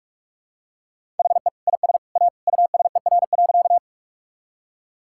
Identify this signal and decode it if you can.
Morse code, heard as a keyed tone, about 44 wpm, 700 Hz. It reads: HE IS A USER.
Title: Sentences from Top 1000 Words - Review Part 5 - 45wpm